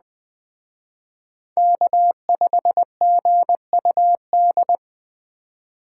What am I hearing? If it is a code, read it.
Morse code, heard as a keyed tone, about 20 wpm, 700 Hz. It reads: K5GUD
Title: Speed Racing - USA 1x3 Call Signs 17wpm